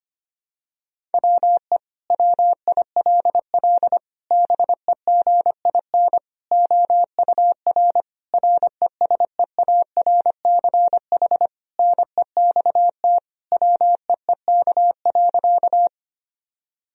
Morse code, heard as a keyed tone, about 25 wpm, 700 Hz. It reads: WE WILL BEGIN OUR RESEARCH NEXT WEEK.
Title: Sentences from Top 1000 Words - Review Part 4 - 25wpm